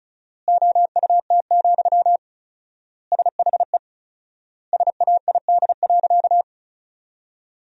Morse code, heard as a keyed tone, about 34 wpm, 700 Hz. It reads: OUT, SHE SAID.
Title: 4x Word-Spacing - Sentences from Top 500 Words - 35wpm